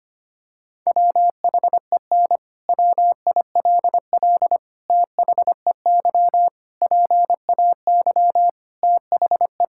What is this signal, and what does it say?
Morse code, heard as a keyed tone, about 25 wpm, 700 Hz. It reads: WHEN WILL THEY PAY THE
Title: Sentences from Top 1000 Words - Review Part 5 - 25wpm